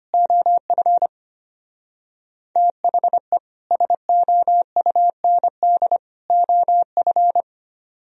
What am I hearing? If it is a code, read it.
Morse code, heard as a keyed tone, about 33 wpm, 700 Hz. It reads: OF THE SOUND OF